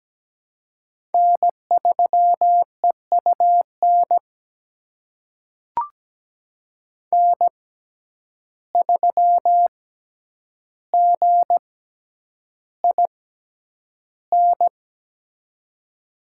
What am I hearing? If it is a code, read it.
Morse code, heard as a keyed tone, about 17 wpm, spaced as for 9 wpm, 700 Hz. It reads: N3EUN N3GIN
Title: USA Call Signs - 5 Characters - 6x Character Spacing - 17wpm